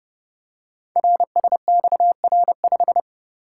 Morse code, heard as a keyed tone, about 30 wpm, 700 Hz. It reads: RSXR5
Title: Code Groups - 5 Letter:Numbers - 30wpm